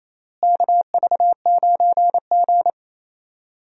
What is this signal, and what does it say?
Morse code, heard as a keyed tone, about 28 wpm, 700 Hz. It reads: KV9G
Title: USA Call Signs - 4 Characters - Repeated 3x - 28wpm